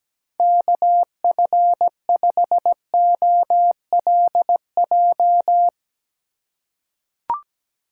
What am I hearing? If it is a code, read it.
Morse code, heard as a keyed tone, about 17 wpm, 700 Hz. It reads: KF5OLJ
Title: USA Call Signs - 6 Characters - 3x Character Spacing - 17wpm